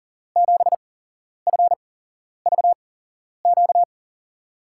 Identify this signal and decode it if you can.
Morse code, heard as a keyed tone, about 38 wpm, spaced as for 11 wpm, 700 Hz. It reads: 7FVQ
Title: USA Call Signs - 6 Characters - 8x Character Spacing - 40wpm